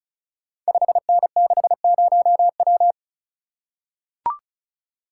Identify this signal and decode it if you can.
Morse code, heard as a keyed tone, about 36 wpm, 690 Hz. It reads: 5N60W
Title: Code Groups - 5 Letter:Numbers - 35wpm